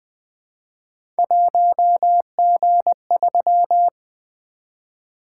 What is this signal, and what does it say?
Morse code, heard as a keyed tone, about 20 wpm, 700 Hz. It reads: 1G3
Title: Three Letter-Number Combinations - 20wpm